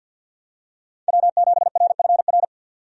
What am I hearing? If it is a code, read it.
Morse code, heard as a keyed tone, about 52 wpm, 700 Hz. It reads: W7RFR
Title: USA Call Signs - 5 Characters - 50wpm